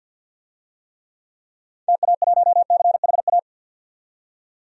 Morse code, heard as a keyed tone, about 48 wpm, 700 Hz. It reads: TA1CHA
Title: DXCC-100 Call Signs - 6 Characters - 50wpm